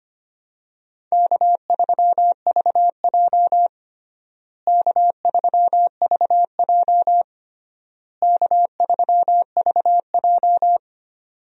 Morse code, heard as a keyed tone, about 25 wpm, 700 Hz. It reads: K3VJ K3VJ K3VJ
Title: USA Call Signs - 4 Characters - Repeated 3x - 25wpm